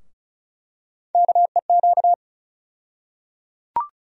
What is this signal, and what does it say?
Morse code, heard as a keyed tone, about 34 wpm, 700 Hz. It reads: KEQ
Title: USA Call Sign Postfixes - 3 Letters 35wpm